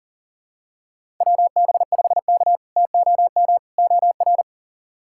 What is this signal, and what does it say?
Morse code, heard as a keyed tone, about 40 wpm, 700 Hz. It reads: WB5K TOM OR
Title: K1USN SST QSO Practice - 40wpm